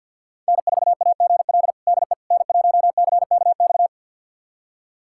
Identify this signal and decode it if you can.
Morse code, heard as a keyed tone, about 47 wpm, 700 Hz. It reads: N4AGL DE N1CKX
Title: USA Callsign DE Callsign - 5 Characters - 50wpm